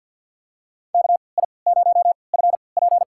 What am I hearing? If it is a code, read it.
Morse code, heard as a keyed tone, about 47 wpm, spaced as for 27 wpm, 700 Hz. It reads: KI0FP
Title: USA Call Signs - 3x Character Spacing - 50wpm